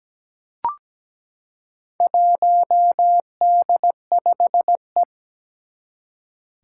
Morse code, heard as a keyed tone, about 17 wpm, 700 Hz. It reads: 1D5E
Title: Four Challenging Letter-Number Combinations 17wpm